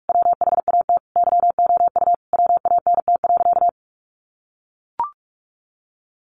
Morse code, heard as a keyed tone, about 44 wpm, 700 Hz. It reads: WHAT YOU WANT.
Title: Sentences from Top 1100 Words 45wpm